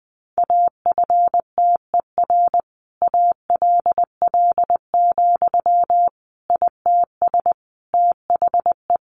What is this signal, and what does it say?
Morse code, heard as a keyed tone, about 19 wpm, 700 Hz. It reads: AFTER ALL, ITS THE
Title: Sentences from Top 700 Words 20wpm